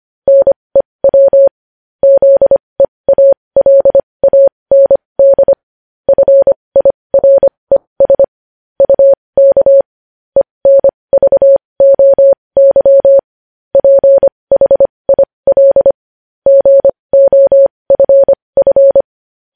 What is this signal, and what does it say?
Morse code, heard as a keyed tone, about 25 wpm, spaced as for 20 wpm, 550 Hz. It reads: NEW ZEALAND FIRES UK ENVOY PHIL GOFF
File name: podcast / News headlines-1x-25x20